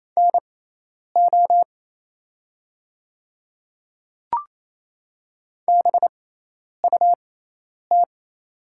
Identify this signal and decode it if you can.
Morse code, heard as a keyed tone, about 27 wpm, spaced as for 10 wpm, 700 Hz. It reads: NO BUT
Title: 6x Character Spacing - Top 100 Words - 28wpm